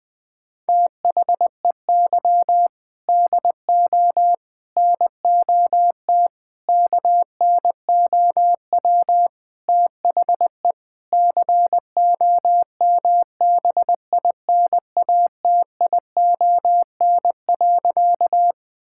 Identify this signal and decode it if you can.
Morse code, heard as a keyed tone, about 20 wpm, 700 Hz. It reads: THEY DO NOT KNOW THE COMBINATION.
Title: Sentences from Top 1000 Words - Review Part 6 - 20wpm